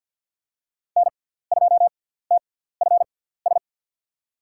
Morse code, heard as a keyed tone, about 47 wpm, spaced as for 17 wpm, 700 Hz. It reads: N2TFS
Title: USA Call Signs - 6x Character Spacing - 50wpm